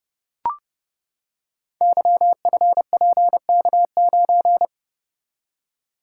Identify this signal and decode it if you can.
Morse code, heard as a keyed tone, about 30 wpm, 700 Hz. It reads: YFPK9
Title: Code Groups - 5 Letter:Numbers - 30wpm